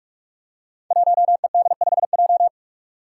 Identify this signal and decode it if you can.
Morse code, heard as a keyed tone, about 44 wpm, 700 Hz. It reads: 1ED5J
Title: Five Challenging Letter-Number Combinations 45wpm